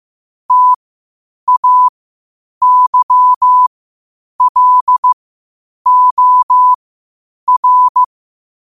Morse code, heard as a keyed tone, about 15 wpm, spaced as for 8 wpm, 1000 Hz. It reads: TAYLOR
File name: qotdfiles / 8wpm